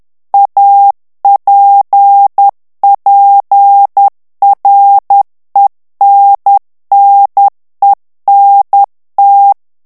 Morse code, heard as a keyed tone, about 10 wpm, 800 Hz. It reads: APPRENNENT
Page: Français - texte en clair 1 - vitesse : 10 mots minute